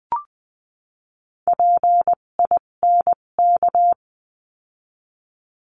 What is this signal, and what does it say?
Morse code, heard as a keyed tone, about 20 wpm, spaced as for 17 wpm, 700 Hz. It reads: PINK
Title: 257 Lesson 42b - Words 14-20wpm